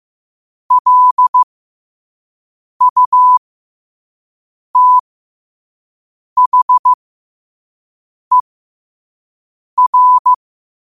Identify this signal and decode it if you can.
Morse code, heard as a keyed tone, about 15 wpm, spaced as for 5 wpm, 1000 Hz. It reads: LUTHER
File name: qotdfiles / 5wpm